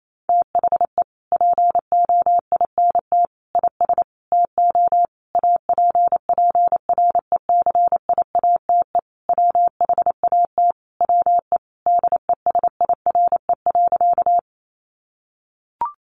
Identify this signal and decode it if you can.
Morse code, heard as a keyed tone, about 28 wpm, 700 Hz. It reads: THE POINT IS TO APPRECIATE WHAT WE DESIRE.
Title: Sentences from Top 1300 Words 28wpm